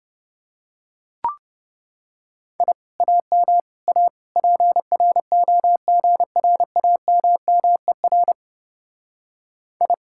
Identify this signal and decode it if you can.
Morse code, heard as a keyed tone, about 30 wpm, 700 Hz. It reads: I AM A PROGRAMMER I